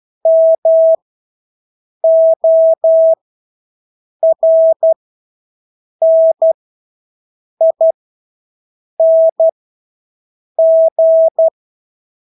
Morse code, heard as a keyed tone, about 12 wpm, spaced as for 6 wpm, 650 Hz. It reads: MORNING
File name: Engelske ord 12-6 wpm